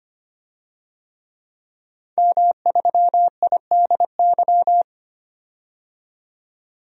Morse code, heard as a keyed tone, about 25 wpm, 700 Hz. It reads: M3IDY
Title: DXCC-100 Call Signs - 5 Characters - 25wpm